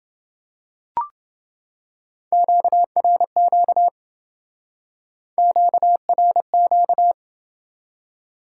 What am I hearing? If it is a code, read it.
Morse code, heard as a keyed tone, about 28 wpm, 700 Hz. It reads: QRQ QRQ